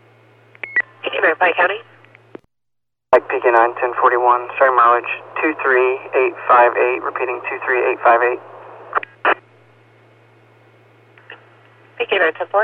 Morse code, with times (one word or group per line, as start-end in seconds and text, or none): none